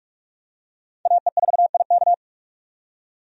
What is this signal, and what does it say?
Morse code, heard as a keyed tone, about 46 wpm, 700 Hz. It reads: AE4IK
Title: USA Call Signs - 5 Characters - 45wpm